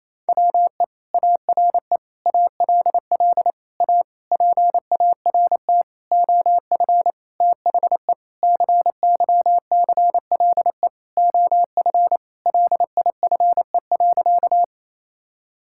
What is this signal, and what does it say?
Morse code, heard as a keyed tone, about 28 wpm, 700 Hz. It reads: WE ARE ALL A PART OF THE CYCLE OF LIFE.
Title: Sentences from Top 1000 Words - Review Part 6 - 28wpm